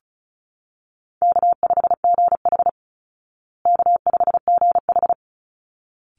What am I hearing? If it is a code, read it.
Morse code, heard as a keyed tone, about 34 wpm, 700 Hz. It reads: K5GH K5GH
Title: Call Signs Repeated 3x - 35wpm